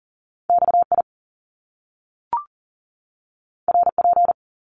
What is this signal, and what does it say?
Morse code, heard as a keyed tone, about 40 wpm, 700 Hz. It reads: XI RP
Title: Top Two Letter Combinations in Words 40wpm